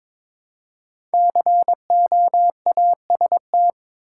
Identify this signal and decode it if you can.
Morse code, heard as a keyed tone, about 22 wpm, 700 Hz. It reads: COAST
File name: 5 Letter Words - First 3 Letters Given - 22wpm